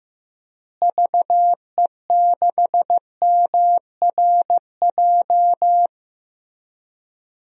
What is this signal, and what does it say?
Morse code, heard as a keyed tone, about 16 wpm, 700 Hz. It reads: VE6MRJ
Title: DXCC-100 Call Signs - 6 Characters - Repeated 2x - 15wpm